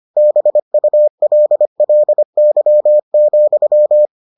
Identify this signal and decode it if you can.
Morse code, heard as a keyed tone, about 25 wpm, 600 Hz. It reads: BULLY,